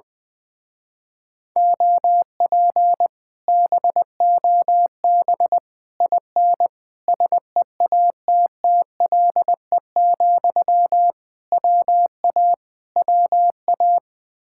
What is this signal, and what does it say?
Morse code, heard as a keyed tone, about 20 wpm, 700 Hz. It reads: OP BOB IN SEATTLE, WA WA
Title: Speed-Racing Common QSO Phrases Encore - 20wpm